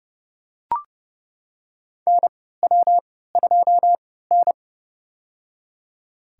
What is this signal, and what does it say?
Morse code, heard as a keyed tone, about 31 wpm, spaced as for 17 wpm, 700 Hz. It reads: NW2N
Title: USA Call Signs - 3x Character Spacing - 30wpm